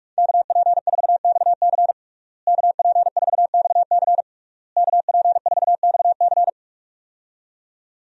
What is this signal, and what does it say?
Morse code, heard as a keyed tone, about 46 wpm, 700 Hz. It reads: KP4XC KP4XC KP4XC
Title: DXCC-100 Call Signs - 5 Characters - Repeated 3x - 45wpm